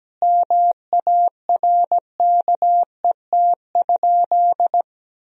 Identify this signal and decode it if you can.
Morse code, heard as a keyed tone, about 17 wpm, 700 Hz. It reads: MARKET?
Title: Sentences from Top 1000 Words - Review Part 3 - 17wpm